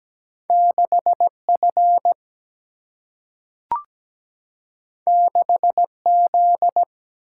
Morse code, heard as a keyed tone, about 17 wpm, 700 Hz. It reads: 6F 6Z
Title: Single-Letter Single-Number 17wpm